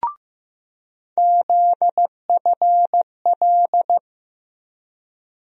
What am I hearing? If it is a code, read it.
Morse code, heard as a keyed tone, about 15 wpm, 700 Hz. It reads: ZFL
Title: USA Call Sign Postfixes - 3 Letters 15wpm